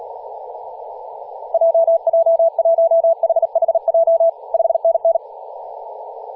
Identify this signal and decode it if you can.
Morse code, heard as a keyed tone, about 38 wpm, 660 Hz, with radio noise. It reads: JJ1HHJ 5NN
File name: posts / 2016-aug-26-2236z-yj0cow-18-073mhz-cw